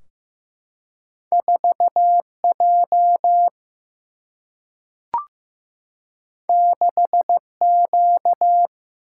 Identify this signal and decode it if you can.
Morse code, heard as a keyed tone, about 15 wpm, 700 Hz. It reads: 4J 6Q
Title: Single-Letter Single-Number 15wpm